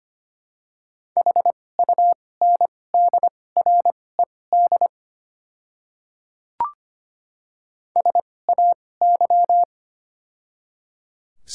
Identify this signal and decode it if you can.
Morse code, heard as a keyed tone, about 25 wpm, spaced as for 18 wpm, 690 Hz. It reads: HUNDRED SAY
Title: 2x Character Spacing - Top 500 Words - 25wpm